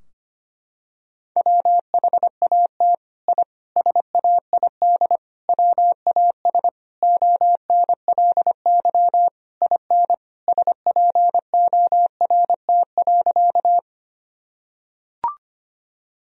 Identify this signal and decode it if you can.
Morse code, heard as a keyed tone, about 25 wpm, 700 Hz. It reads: WHAT I SAID WAS ONLY IN SPORT.
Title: Sentences from Top 1000 Words - Review Part 4 - 25wpm